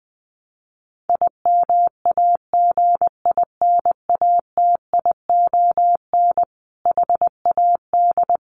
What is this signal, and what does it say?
Morse code, heard as a keyed tone, about 19 wpm, 700 Hz. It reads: IMAGINATION HAD
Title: Sentences from Top 1300 Words 20wpm